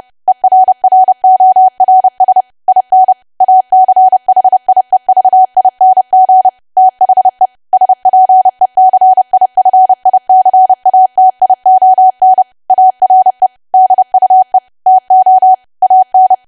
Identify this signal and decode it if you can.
Morse code, heard as a keyed tone, about 30 wpm, 750 Hz. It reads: ERRORS IN ACHIEVING THE SPECIFICATION ARE DUE TO AN
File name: cwa / W1AW-Level3 / 30WPM